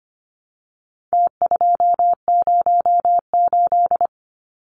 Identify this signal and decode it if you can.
Morse code, heard as a keyed tone, about 25 wpm, 700 Hz. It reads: T208
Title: Numbers - Contest Style - 4-Digit 25wpm